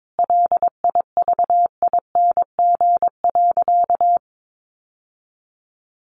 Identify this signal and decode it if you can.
Morse code, heard as a keyed tone, about 22 wpm, 700 Hz. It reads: LIVING.
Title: Sentences from Top 1500 Words 22wpm